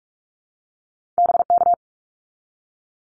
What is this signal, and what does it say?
Morse code, heard as a keyed tone, about 44 wpm, 700 Hz. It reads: BK
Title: Abbreviations 45wpm